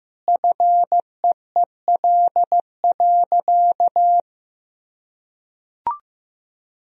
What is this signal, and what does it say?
Morse code, heard as a keyed tone, about 15 wpm, 700 Hz. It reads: FEEL.
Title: Sentences from Top 1000 Words - Review Part 3 - 15wpm